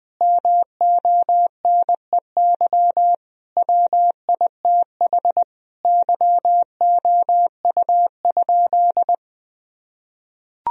Morse code, heard as a keyed tone, about 20 wpm, 700 Hz. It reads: MONEY WITH YOU?
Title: Sentences from Top 1000 Words - Review Part 1 - 20wpm